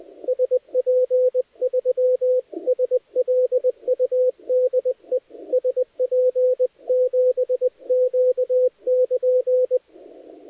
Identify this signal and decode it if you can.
Morse code, heard as a keyed tone, about 20 wpm, 500 Hz, with radio noise. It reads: SP3SLU DE SP7Q <KN>
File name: dlazaawansowanych / nagraniazpasma / titawaNETcontrol